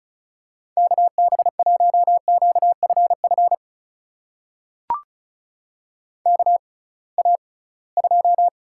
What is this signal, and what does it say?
Morse code, heard as a keyed tone, about 36 wpm, spaced as for 23 wpm, 700 Hz. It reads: KB1QFF KA2
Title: USA Call Signs - 6 Characters - 6x Character Spacing - 35wpm